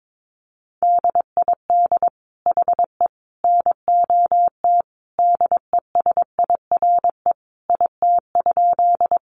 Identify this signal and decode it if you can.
Morse code, heard as a keyed tone, about 22 wpm, 700 Hz. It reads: DID HE NOT DESIRE IT?
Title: Sentences from Top 1100 Words 22wpm